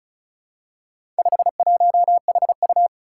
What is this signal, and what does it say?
Morse code, heard as a keyed tone, about 36 wpm, 700 Hz. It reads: 51HU
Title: Four Challenging Letter-Number Combinations 35wpm